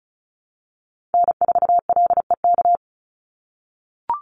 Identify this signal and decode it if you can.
Morse code, heard as a keyed tone, about 34 wpm, 700 Hz. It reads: N4LEK
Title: USA Call Signs 35wpm